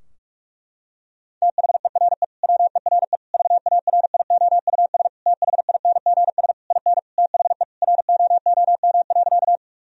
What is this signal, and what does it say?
Morse code, heard as a keyed tone, about 46 wpm, 700 Hz. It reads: THERE WERE VARIOUS THINGS IN THE ROOM.
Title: Sentences from Top 1000 Words - Review Part 4 - 45wpm